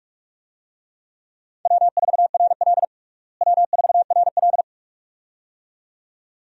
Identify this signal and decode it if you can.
Morse code, heard as a keyed tone, about 44 wpm, 700 Hz. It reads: W4RL W4RL
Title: USA Call Signs - 4 Characters - Repeated 2x - 45wpm